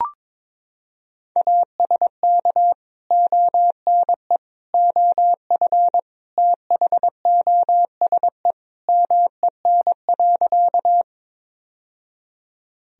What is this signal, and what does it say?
Morse code, heard as a keyed tone, about 22 wpm, 700 Hz. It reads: ASK ONE OF THOSE MEN.
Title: Sentences from Top 1000 Words - Review Part 1 - 22wpm